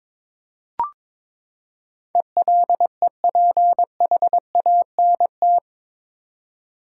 Morse code, heard as a keyed tone, about 22 wpm, 700 Hz. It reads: ELEPHANT
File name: practice / 22wpm / Eight Letter Words - 22wpm